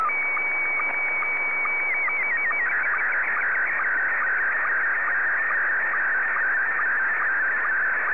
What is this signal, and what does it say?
SSTV (slow-scan television), Scottie 1.